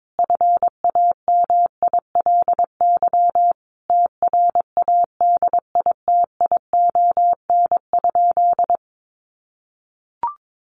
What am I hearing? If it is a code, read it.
Morse code, heard as a keyed tone, about 22 wpm, 700 Hz. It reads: FAMILY TRADITION?
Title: Sentences from Top 1500 Words 22wpm